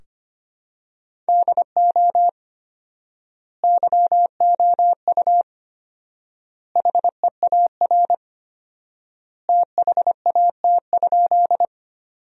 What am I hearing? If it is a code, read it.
Morse code, heard as a keyed tone, about 25 wpm, 700 Hz. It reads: DO YOU HEAR THAT?
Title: 4x Word-Spacing - Sentences from Top 300 Words - 25wpm